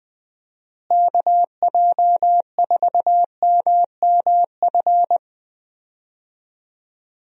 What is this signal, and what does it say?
Morse code, heard as a keyed tone, about 20 wpm, 700 Hz. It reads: KJ4MMF
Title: USA Call Signs - 6 Characters - Repeated 3x - 20wpm